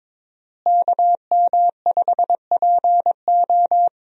Morse code, heard as a keyed tone, about 22 wpm, 700 Hz. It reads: KM5PO
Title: USA Call Signs - 5 Characters - Repeated 2x - 22wpm